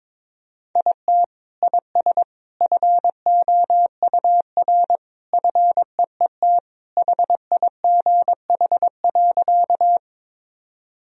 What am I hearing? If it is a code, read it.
Morse code, heard as a keyed tone, about 22 wpm, 700 Hz. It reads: IT IS FOUR FEET HIGH.
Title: Sentences from Top 1000 Words - Review Part 1 - 22wpm